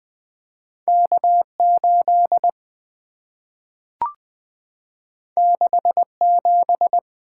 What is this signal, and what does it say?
Morse code, heard as a keyed tone, about 20 wpm, 700 Hz. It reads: K 8 6 7
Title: Two Letter-Number Combinations - 20wpm